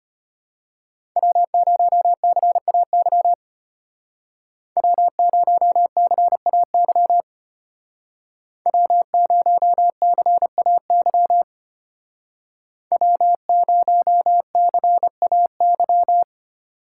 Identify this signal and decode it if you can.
Morse code, heard as a keyed tone, about 30 wpm, 700 Hz. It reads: W0CAY W0CAY W0CAY W0CAY
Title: Speed Racing - USA Call Signs 25wpm